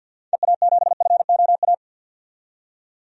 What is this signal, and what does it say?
Morse code, heard as a keyed tone, about 52 wpm, 700 Hz. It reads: EA7FOA
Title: DXCC-100 Call Signs - 6 Characters - 50wpm